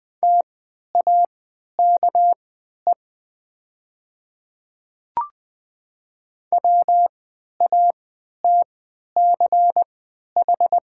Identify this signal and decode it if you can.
Morse code, heard as a keyed tone, about 20 wpm, spaced as for 11 wpm, 700 Hz. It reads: TAKE WATCH
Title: 3x Character Spacing - Top 500 Words - 20wpm